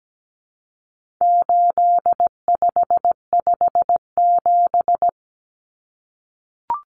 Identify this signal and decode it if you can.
Morse code, heard as a keyed tone, about 17 wpm, 700 Hz. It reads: 8557
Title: Numbers - 4-Digit 17wpm